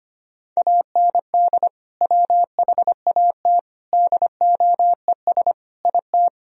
Morse code, heard as a keyed tone, about 25 wpm, 700 Hz. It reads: AND WHAT DOES IT
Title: Sentences from Top 1000 Words - Review Part 4 - 25wpm